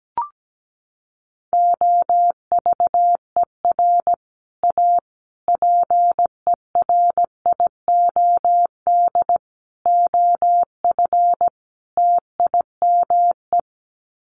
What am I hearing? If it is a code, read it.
Morse code, heard as a keyed tone, about 17 wpm, 700 Hz. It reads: OVER A PERIOD OF TIME
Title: Sets of 5 Words Encore2 17wpm